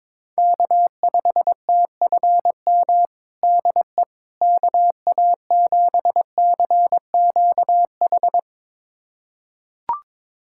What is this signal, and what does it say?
Morse code, heard as a keyed tone, about 22 wpm, 700 Hz. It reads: K5TFM DE KA7CQH
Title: USA Callsign DE Callsign 22wpm